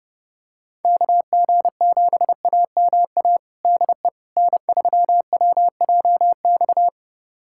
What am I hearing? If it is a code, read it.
Morse code, heard as a keyed tone, about 30 wpm, 700 Hz. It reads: KG7AMA DE N3WJX
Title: USA Callsign DE Callsign 30wpm